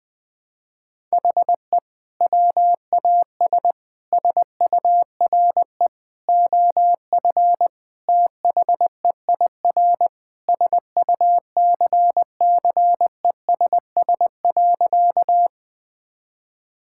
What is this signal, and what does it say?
Morse code, heard as a keyed tone, about 20 wpm, 700 Hz. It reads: HE WAS SURE OF THEIR SUCCESS.
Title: Sentences from Top 1000 Words - Review Part 6 - 20wpm